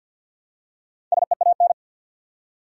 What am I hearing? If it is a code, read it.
Morse code, heard as a keyed tone, about 53 wpm, 700 Hz. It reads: SEAN
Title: USA Common Names - 50wpm